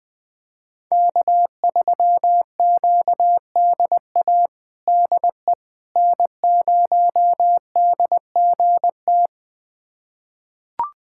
Morse code, heard as a keyed tone, about 21 wpm, 700 Hz. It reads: K3QDA DE N0DGT
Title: USA Callsign DE Callsign - 5 Characters - 20wpm